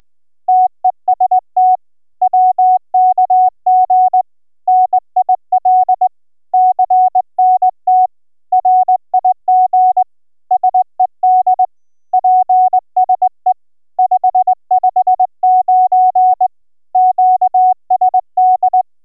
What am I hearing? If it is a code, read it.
Morse code, heard as a keyed tone, about 22 wpm, 740 Hz. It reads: TEST WKG NIL CNT RIG SED PSE 559 QSD